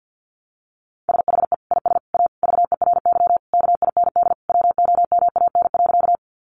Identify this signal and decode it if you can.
Morse code, heard as a keyed tone, about 50 wpm, 700 Hz. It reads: SHE IS A VERY KIND WOMAN.
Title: Sentences from Top 1100 Words 50wpm